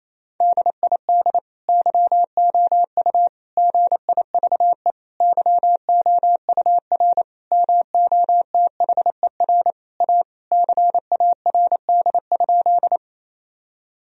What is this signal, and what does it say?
Morse code, heard as a keyed tone, about 28 wpm, 700 Hz. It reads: DID YOU GIVE YOUR MOTHER A CARD?
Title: Sentences from Top 1000 Words - Review Part 4 - 28wpm